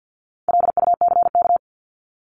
Morse code, heard as a keyed tone, about 50 wpm, 700 Hz. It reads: LUCK
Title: Top 1251-1500 Words 50wpm